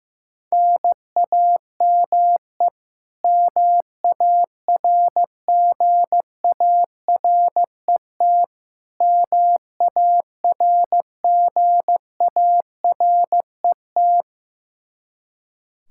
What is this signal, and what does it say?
Morse code, heard as a keyed tone, about 15 wpm, 700 Hz. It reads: NAME MARGARET MARGARET
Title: Practice QSO - Exchange 1 - Beginning - 15wpm